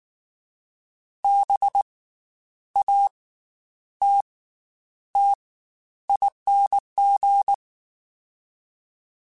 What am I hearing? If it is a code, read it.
Morse code, heard as a keyed tone, about 19 wpm, 800 Hz. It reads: B A T T ING
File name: Special INGs / Special INGs-mp3s / Slow ing @18